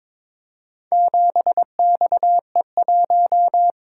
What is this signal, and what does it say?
Morse code, heard as a keyed tone, about 22 wpm, 700 Hz. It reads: 7XE1
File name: practice / 22wpm / Four Letter-Number Combinations - 22wpm